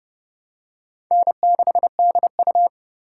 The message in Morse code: N6DU

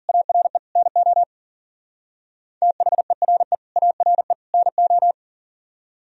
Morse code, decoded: ARE NO THERE ARE NO

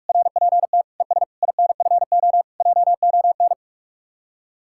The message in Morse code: RPT ES INFO JON

